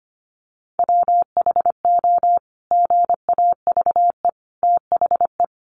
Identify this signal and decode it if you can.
Morse code, heard as a keyed tone, about 25 wpm, 700 Hz. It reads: WHO GAVE THE
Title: Sentences from Top 900 Words 25wpm